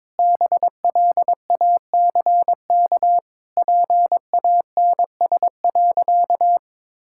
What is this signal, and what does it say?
Morse code, heard as a keyed tone, about 22 wpm, 700 Hz. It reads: BLACK PANS.
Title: Pangram Challenge v2 - 22wpm